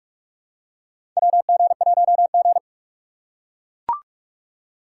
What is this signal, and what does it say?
Morse code, heard as a keyed tone, about 42 wpm, 700 Hz. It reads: WG1G
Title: USA Call Signs - 4 Characters - 6x Character Spacing - 45wpm